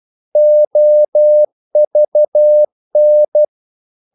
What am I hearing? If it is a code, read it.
Morse code, heard as a keyed tone, about 12 wpm, 600 Hz. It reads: OVN
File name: Korte ord DK 12wpm